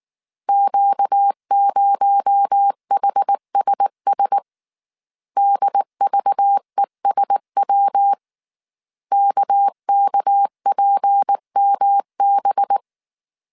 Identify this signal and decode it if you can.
Morse code, heard as a keyed tone, about 19 wpm, 800 Hz. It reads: Q0HSS DVESW KKPMB